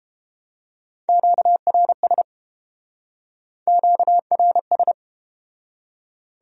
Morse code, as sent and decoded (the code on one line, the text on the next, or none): --.- .-. ... / --.- .-. ...
QRS QRS